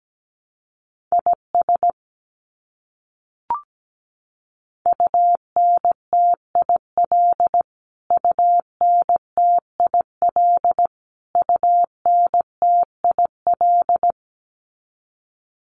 Morse code, heard as a keyed tone, about 17 wpm, 700 Hz. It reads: IS UNTIL UNTIL UNTIL
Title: Top 500 Words Repeated 17wpm